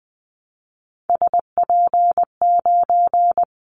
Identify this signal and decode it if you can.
Morse code, heard as a keyed tone, about 20 wpm, 700 Hz. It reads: SP9
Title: Callsign Prefixes for Top 25 DXCC - 20wpm